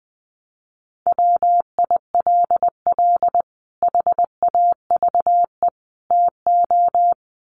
Morse code, heard as a keyed tone, about 20 wpm, 700 Hz. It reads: WILL HAVE TO
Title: Sets of 3 Words 20wpm